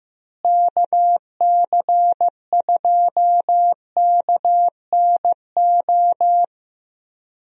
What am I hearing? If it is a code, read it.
Morse code, heard as a keyed tone, about 16 wpm, 700 Hz. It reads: KC2KNO
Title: USA Call Signs - 6 Characters - Repeated 2x - 15wpm